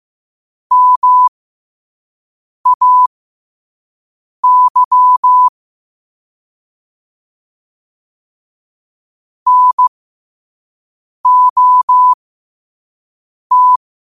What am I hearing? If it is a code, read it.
Morse code, heard as a keyed tone, about 15 wpm, spaced as for 5 wpm, 1000 Hz. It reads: MAY NOT
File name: qotdfiles / 5wpm